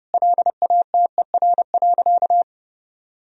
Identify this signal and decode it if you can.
Morse code, heard as a keyed tone, about 30 wpm, 700 Hz. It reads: LATER.
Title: Sentences from Top 1000 Words - Review Part 3 - 30wpm